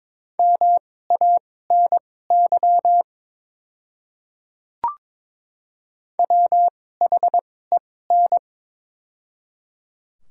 Morse code, heard as a keyed tone, about 22 wpm, spaced as for 16 wpm, 700 Hz. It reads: MANY WHEN
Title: 2x Character Spacing - Top 200 Words - 22wpm